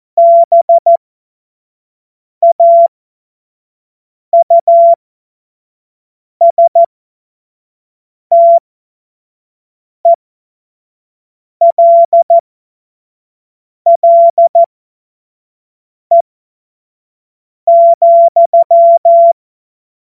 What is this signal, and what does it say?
Morse code, heard as a keyed tone, about 14 wpm, spaced as for 5 wpm, 680 Hz. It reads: BAUSTELLE,
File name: Morsekurs / Witze / Witz 30. Dezember 2025 25 F70